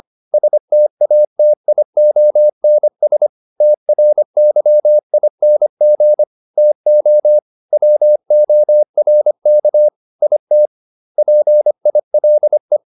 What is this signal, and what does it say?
Morse code, heard as a keyed tone, about 25 wpm, 600 Hz. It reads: STATIONS TRYING TO WORK IT PILE